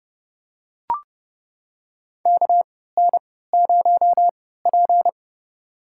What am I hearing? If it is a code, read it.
Morse code, heard as a keyed tone, about 30 wpm, spaced as for 17 wpm, 700 Hz. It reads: KN0P
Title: USA Call Signs - 4 Characters - 3x Character Spacing - 30wpm